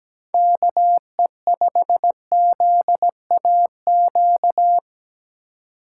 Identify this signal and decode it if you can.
Morse code, heard as a keyed tone, about 17 wpm, 700 Hz. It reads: KE5ZAQ
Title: USA Call Signs - 3x Character Spacing - 17wpm